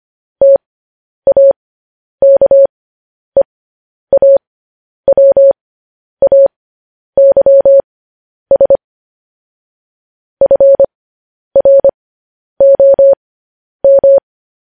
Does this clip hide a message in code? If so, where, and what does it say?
Yes: Morse code from 0.4 to 14.2 s, heard as a keyed tone, about 25 wpm, spaced as for 10 wpm, 550 Hz. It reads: TAKEAWAYS FROM